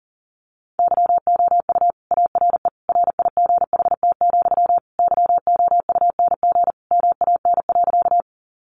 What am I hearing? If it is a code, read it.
Morse code, heard as a keyed tone, about 40 wpm, 700 Hz. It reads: YOU ARE RIGHT, YOUNG MAN.
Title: Sentences from Top 400 Words 40wpm